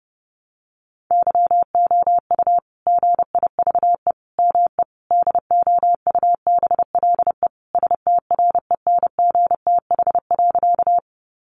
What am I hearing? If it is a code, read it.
Morse code, heard as a keyed tone, about 30 wpm, 700 Hz. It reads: YOU GIVE ME DOUBLE STRENGTH.
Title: Sentences from Top 1400 Words 30wpm